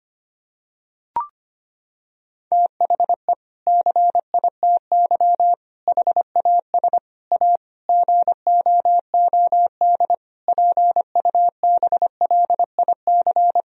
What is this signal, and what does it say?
Morse code, heard as a keyed tone, about 25 wpm, 700 Hz. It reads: THE CITY HAS A GOOD PUBLIC
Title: Sentences from Top 1000 Words - Review Part 4 - 25wpm